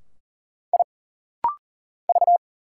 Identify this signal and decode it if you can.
Morse code, heard as a keyed tone, about 40 wpm, 700 Hz. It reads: I V
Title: Single Letters - Rapid-Fire 40wpm